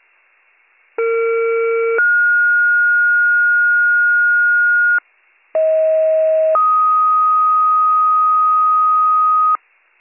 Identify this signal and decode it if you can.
Two-tone page, heard as a one-second tone, then a three-second tone, about 460 then 1430 Hz, about 640 then 1250 Hz, with noise. It is heard from 1.0 to 9.6 s.